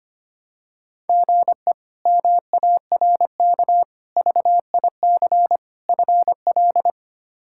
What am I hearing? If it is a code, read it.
Morse code, heard as a keyed tone, about 25 wpm, 700 Hz. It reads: GE MARK VIC FL
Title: K1USN SST QSO Practice - 25wpm